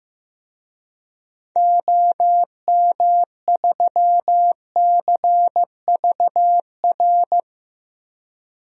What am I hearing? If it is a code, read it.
Morse code, heard as a keyed tone, about 15 wpm, 700 Hz. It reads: OM3CVR
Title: DXCC-100 Call Signs - 6 Characters - 15wpm